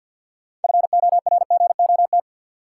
Morse code, heard as a keyed tone, about 46 wpm, 700 Hz. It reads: FORGOT
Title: Sentences from Top 1400 Words - New Word Drill Alt - 50wpm